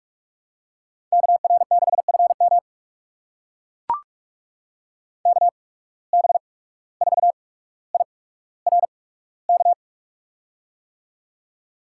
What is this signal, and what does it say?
Morse code, heard as a keyed tone, about 42 wpm, spaced as for 18 wpm, 700 Hz. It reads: KR6FM KB4IRK
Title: USA Call Signs - 8x Character Spacing - 45wpm